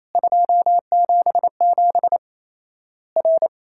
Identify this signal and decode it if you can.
Morse code, heard as a keyed tone, about 28 wpm, 700 Hz. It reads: 277 R
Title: Worked All Europe DX Contest - QTC Exchange - 28wpm